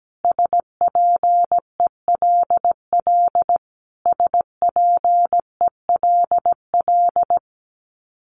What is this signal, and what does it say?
Morse code, heard as a keyed tone, about 17 wpm, 700 Hz. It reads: SPELL SPELL
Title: Top 300 Words Repeated 17wpm